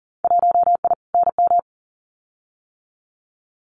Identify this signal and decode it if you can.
Morse code, heard as a keyed tone, about 37 wpm, 700 Hz. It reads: 1I NM
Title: Winter Field Day Exchanges Encore 40wpm